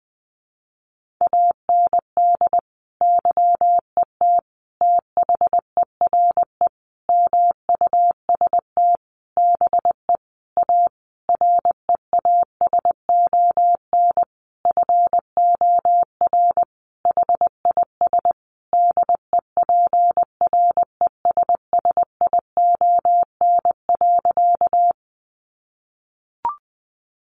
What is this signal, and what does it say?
Morse code, heard as a keyed tone, about 19 wpm, 700 Hz. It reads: AND YET THERE MUST BE A REASON FOR HIS DEPRESSION.
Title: Sentences from Top 1300 Words 20wpm